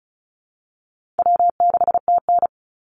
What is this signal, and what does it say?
Morse code, heard as a keyed tone, about 34 wpm, 700 Hz. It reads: W6TN
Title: Easy Call Signs Repeated 2x - 35wpm